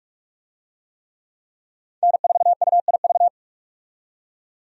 Morse code, heard as a keyed tone, about 46 wpm, 700 Hz. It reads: N4UIV
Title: USA Call Signs - 5 Characters - 45wpm